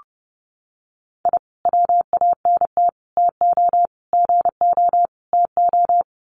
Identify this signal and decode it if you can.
Morse code, heard as a keyed tone, about 30 wpm, 700 Hz. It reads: I WANT TO GO TO
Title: Sets of 5 Words Encore 30wpm